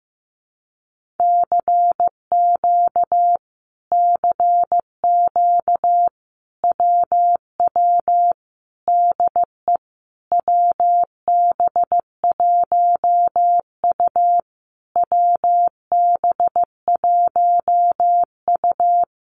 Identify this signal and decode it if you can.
Morse code, heard as a keyed tone, about 15 wpm, 700 Hz. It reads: CQ CQ WW DE WB1U WB1U
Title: Common QSO Phrases Encore 15wpm